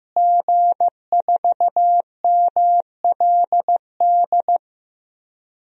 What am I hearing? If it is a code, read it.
Morse code, heard as a keyed tone, about 15 wpm, 700 Hz. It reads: G4MLD
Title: DXCC-100 Call Signs - 5 Characters - Repeated 3x - 15wpm